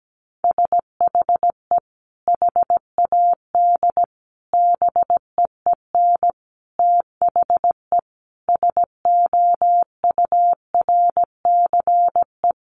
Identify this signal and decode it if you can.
Morse code, heard as a keyed tone, about 17 wpm, 700 Hz. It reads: SHE HAD BEEN THE SOURCE
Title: Sentences from Top 1200 Words 17wpm